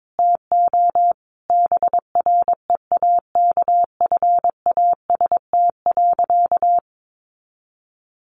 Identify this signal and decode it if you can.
Morse code, heard as a keyed tone, about 22 wpm, 700 Hz. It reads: TO BREAKFAST.
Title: Sentences from Top 1500 Words 22wpm